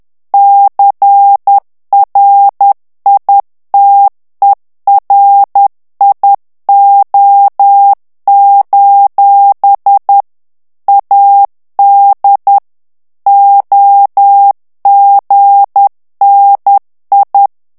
Morse code, heard as a keyed tone, about 10 wpm, 800 Hz. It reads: CRITERIO: AD OGNI
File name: ITALIEN texte clair 2 vitesse 10 mots minute